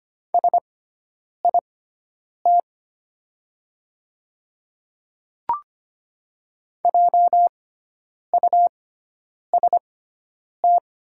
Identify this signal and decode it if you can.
Morse code, heard as a keyed tone, about 25 wpm, spaced as for 9 wpm, 700 Hz. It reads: SIT JUST
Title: 6x Character Spacing - Top 500 Words - 25wpm